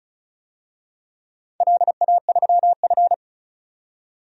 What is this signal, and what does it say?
Morse code, heard as a keyed tone, about 34 wpm, 700 Hz. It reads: LA3F
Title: DXCC-100 Call Signs - 4 Characters - 35wpm